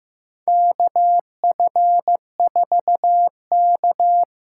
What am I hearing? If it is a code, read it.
Morse code, heard as a keyed tone, about 15 wpm, 700 Hz. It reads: KF4K
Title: USA Call Signs - 6 Characters - 6x Character Spacing - 15wpm